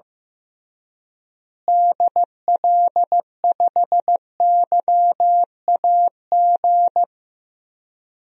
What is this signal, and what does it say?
Morse code, heard as a keyed tone, about 15 wpm, 700 Hz. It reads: DL5YAG
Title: DXCC-100 Call Signs - 6 Characters - Repeated 3x - 15wpm